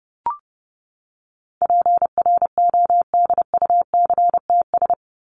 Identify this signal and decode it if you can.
Morse code, heard as a keyed tone, about 30 wpm, 700 Hz. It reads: PRODUCTS
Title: ES:S Suffix - Top 100 Words with Base Words 30wpm